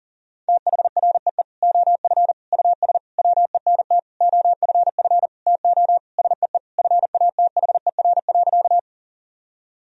Morse code, heard as a keyed tone, about 40 wpm, 700 Hz. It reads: THREE OF US WENT OFF TO SEE FATHER.
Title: Sentences from Top 1000 Words - Review Part 1 - 40wpm